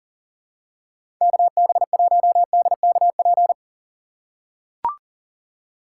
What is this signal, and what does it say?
Morse code, heard as a keyed tone, about 38 wpm, 700 Hz. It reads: KB1DKP
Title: USA Call Signs - 6 Characters - 6x Character Spacing - 40wpm